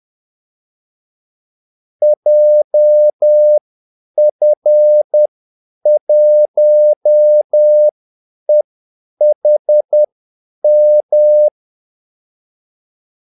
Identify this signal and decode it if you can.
Morse code, heard as a keyed tone, about 10 wpm, spaced as for 8 wpm, 600 Hz. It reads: JF1EHM
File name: Callsigns-10-8wpm